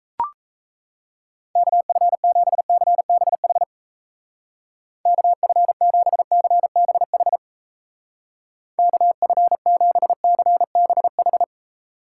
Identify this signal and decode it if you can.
Morse code, heard as a keyed tone, about 35 wpm, 700 Hz. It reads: KF7CBH KF7CBH KF7CBH
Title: Speed Racing - USA Call Signs 28wpm